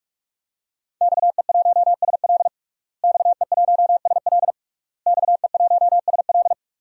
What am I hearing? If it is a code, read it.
Morse code, heard as a keyed tone, about 41 wpm, 700 Hz. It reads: XE1SL XE1SL XE1SL
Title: DXCC-100 Call Signs - 5 Characters - Repeated 3x - 45wpm